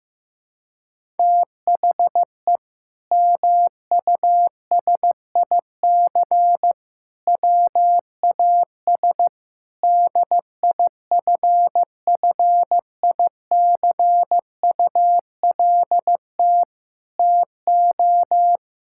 Morse code, heard as a keyed tone, about 15 wpm, 700 Hz. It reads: THE MUSIC WAS DIFFICULT TO
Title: Sentences from Top 1000 Words - Review Part 6 - 15wpm